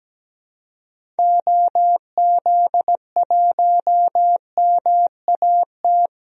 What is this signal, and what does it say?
Morse code, heard as a keyed tone, about 17 wpm, 700 Hz. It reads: OZ1MAT
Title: DXCC-100 Call Signs - 6 Characters - 17wpm